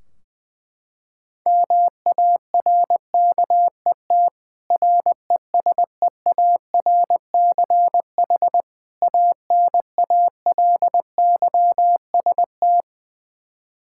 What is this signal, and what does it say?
Morse code, heard as a keyed tone, about 20 wpm, 700 Hz. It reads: MARKET RESEARCH ANALYST
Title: Practice QSO - Exchange 3 - More Info - 20wpm